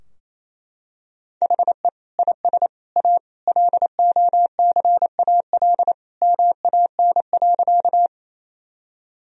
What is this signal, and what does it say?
Morse code, heard as a keyed tone, about 28 wpm, 700 Hz. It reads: HE IS A LOCAL MAN.
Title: Sentences from Top 1000 Words - Review Part 3 - 28wpm